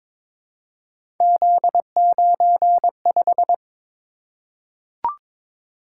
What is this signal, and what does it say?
Morse code, heard as a keyed tone, about 22 wpm, 700 Hz. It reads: Z95
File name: practice / 22wpm / Three Letter-Number Combinations - 22wpm